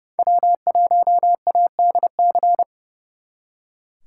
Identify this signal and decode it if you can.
Morse code, heard as a keyed tone, about 37 wpm, 700 Hz. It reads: W1ADC
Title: Speed Racing - USA 1x3 Call Signs 30wpm